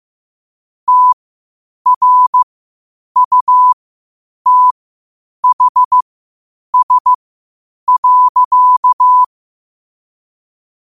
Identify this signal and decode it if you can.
Morse code, heard as a keyed tone, about 15 wpm, spaced as for 8 wpm, 1000 Hz. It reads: TRUTHS.
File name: qotdfiles / 8wpm